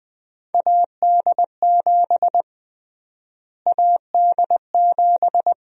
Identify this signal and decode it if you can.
Morse code, heard as a keyed tone, about 20 wpm, 700 Hz. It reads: AD7 AD7
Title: USA Call Sign Prefixes - Repeated 2x - 20wpm